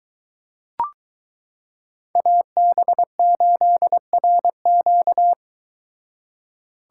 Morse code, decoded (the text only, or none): AB8RQ